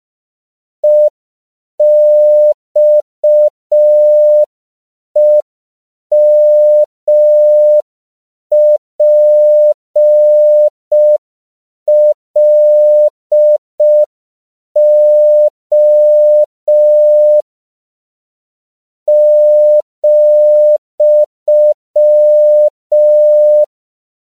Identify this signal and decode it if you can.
Morse code, heard as a keyed tone, about 5 wpm, 600 Hz. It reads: EXEMPLO ,
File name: 5PPM